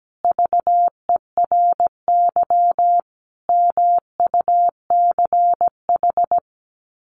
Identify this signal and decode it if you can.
Morse code, heard as a keyed tone, about 17 wpm, 700 Hz. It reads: VERY MUCH
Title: Sets of 5 Words 17wpm